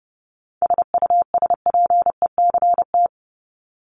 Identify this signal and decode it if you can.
Morse code, heard as a keyed tone, about 30 wpm, 700 Hz. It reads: SUSPECT
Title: Top 1751-2000 Words 30wpm